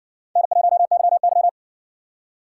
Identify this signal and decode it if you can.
Morse code, heard as a keyed tone, about 60 wpm, 700 Hz. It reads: N1YY